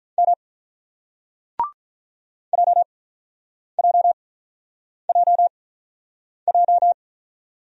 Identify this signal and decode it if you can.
Morse code, heard as a keyed tone, about 46 wpm, spaced as for 9 wpm, 700 Hz. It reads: M JJJJ